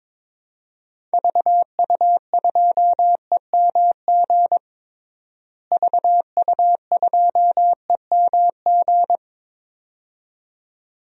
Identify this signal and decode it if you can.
Morse code, heard as a keyed tone, about 22 wpm, 700 Hz. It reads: VU2EMG VU2EMG
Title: DXCC-100 Call Signs - 6 Characters - Repeated 2x - 22wpm